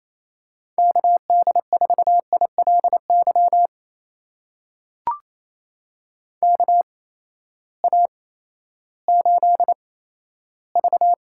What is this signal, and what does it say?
Morse code, heard as a keyed tone, about 27 wpm, spaced as for 14 wpm, 700 Hz. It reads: KD4ILY KA8V
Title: USA Call Signs - 6 Characters - 8x Character Spacing - 28wpm